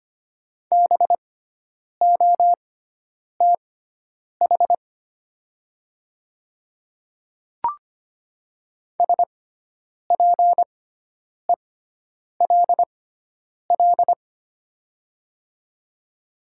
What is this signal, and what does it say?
Morse code, heard as a keyed tone, about 25 wpm, spaced as for 9 wpm, 700 Hz. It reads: BOTH SPELL
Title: 6x Character Spacing - Top 300 Words - 25wpm